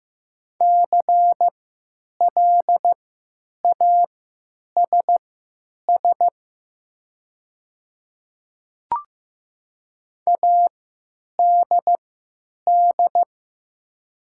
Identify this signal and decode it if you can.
Morse code, heard as a keyed tone, about 15 wpm, spaced as for 9 wpm, 700 Hz. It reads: CLASS ADD
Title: 3x Character Spacing - Top 500 Words - 15wpm